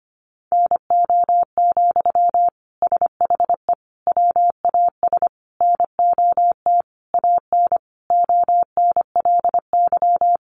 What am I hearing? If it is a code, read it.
Morse code, heard as a keyed tone, about 25 wpm, 700 Hz. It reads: NO, SHE WAS NOT AN ONLY
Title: Sentences from Top 800 Words 25wpm